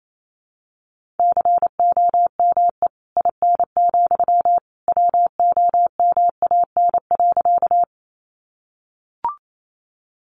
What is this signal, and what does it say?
Morse code, heard as a keyed tone, about 28 wpm, 700 Hz. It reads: COME IN, WOMAN.
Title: Sentences from Top 1100 Words 28wpm